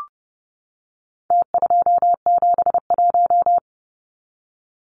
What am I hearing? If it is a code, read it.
Morse code, heard as a keyed tone, about 30 wpm, 700 Hz. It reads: T271
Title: Numbers - Contest Style - 4-Digit 30wpm